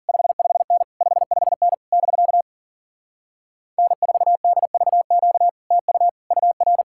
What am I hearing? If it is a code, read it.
Morse code, heard as a keyed tone, about 52 wpm, 700 Hz. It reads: 55N 55N <BK> N4DVQ TU UR